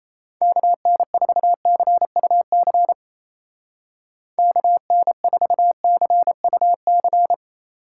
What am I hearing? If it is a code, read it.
Morse code, heard as a keyed tone, about 34 wpm, 700 Hz. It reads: KN4CUC KN4CUC
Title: Speed Racing - USA 2x3 Call Signs 28wpm